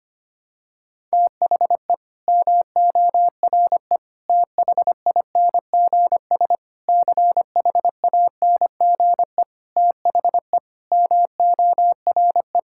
Morse code, heard as a keyed tone, about 25 wpm, 700 Hz. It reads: THE MORE THINGS CHANGE THE MORE
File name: Sets of 9 Words - 25wpm